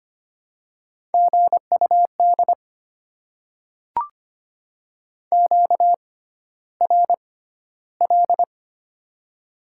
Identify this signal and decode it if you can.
Morse code, heard as a keyed tone, about 25 wpm, spaced as for 13 wpm, 700 Hz. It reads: GUD QRL
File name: Top 20 QSO Elements - 6x Character Spacing - 25wpm